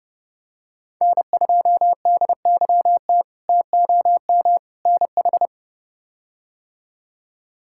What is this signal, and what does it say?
Morse code, heard as a keyed tone, about 30 wpm, 700 Hz. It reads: N2DYT TOM NH
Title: K1USN SST QSO Practice - 30wpm